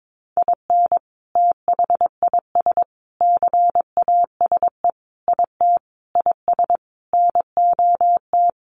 Morse code, heard as a keyed tone, about 22 wpm, 700 Hz. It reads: IN THIS CASE IT IS NOT
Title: Sentences from Top 1400 Words 22wpm